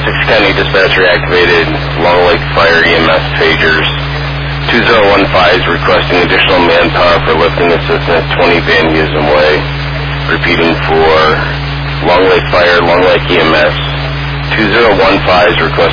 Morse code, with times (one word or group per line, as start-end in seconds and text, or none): none